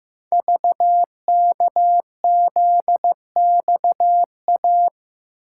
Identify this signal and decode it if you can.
Morse code, heard as a keyed tone, about 15 wpm, 700 Hz. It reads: VKZXA
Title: Code Groups - 5 Letters - 15wpm